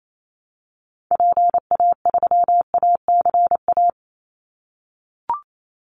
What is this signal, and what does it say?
Morse code, heard as a keyed tone, about 28 wpm, 700 Hz. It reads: PA3ACA
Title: Call Signs - 28wpm